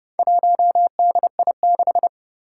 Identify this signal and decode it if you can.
Morse code, heard as a keyed tone, about 29 wpm, 700 Hz. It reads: 1DI6
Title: Four Challenging Letter-Number Combinations 30wpm